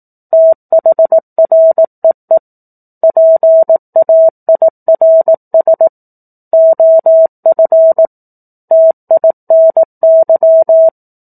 Morse code, heard as a keyed tone, about 18 wpm, 650 Hz. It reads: THREE PAIRS OF TINY
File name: 18 WPM